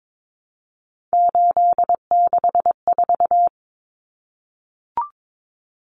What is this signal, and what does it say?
Morse code, heard as a keyed tone, about 22 wpm, 700 Hz. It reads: 864
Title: Common Cut Numbers - 3-Digit 22wpm